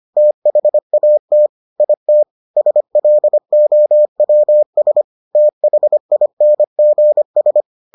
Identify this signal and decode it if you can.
Morse code, heard as a keyed tone, about 25 wpm, 600 Hz. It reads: THAT IT SLOWS THINGS